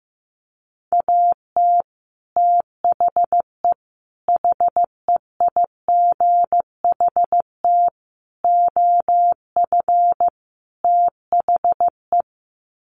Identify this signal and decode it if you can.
Morse code, heard as a keyed tone, about 14 wpm, 700 Hz. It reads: AT THE HEIGHT OF THE
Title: Sets of 5 Words 15wpm